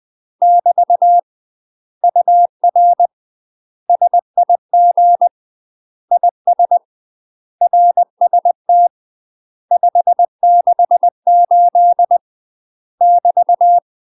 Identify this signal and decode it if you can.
Morse code, heard as a keyed tone, about 20 wpm, 700 Hz. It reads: <BT> UR SIG IS RST 568 <BT>
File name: UK MASTER QSOs TRAINING / 20WPM / UK MASTER QSOs 0024